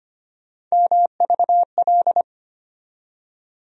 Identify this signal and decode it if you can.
Morse code, heard as a keyed tone, about 25 wpm, 700 Hz. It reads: MVL
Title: USA Call Sign Postfixes - 3 Letters 25wpm